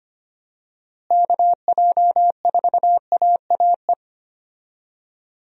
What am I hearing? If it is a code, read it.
Morse code, heard as a keyed tone, about 25 wpm, 700 Hz. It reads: KJ4AAE
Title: USA Call Signs - 6 Characters - Repeated 2x - 25wpm